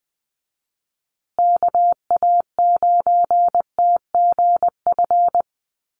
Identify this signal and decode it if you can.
Morse code, heard as a keyed tone, about 20 wpm, 700 Hz. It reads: KA9TGF
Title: USA Call Signs Repeated 2x 20wpm